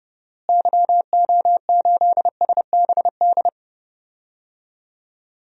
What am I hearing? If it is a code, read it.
Morse code, heard as a keyed tone, about 32 wpm, 700 Hz. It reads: YO8SBD
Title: DXCC-100 Call Signs - 6 Characters - Repeated 3x - 30wpm